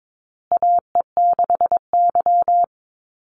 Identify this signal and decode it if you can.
Morse code, heard as a keyed tone, about 22 wpm, 700 Hz. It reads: AE6Y
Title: Call Signs Repeated 2x - 22wpm